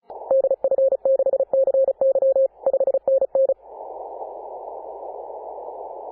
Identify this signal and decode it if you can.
Morse code, heard as a keyed tone, about 34 wpm, 540 Hz. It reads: DF6CY 5NN
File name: Apr/29 2042z JD1BLY 14.011MHz CW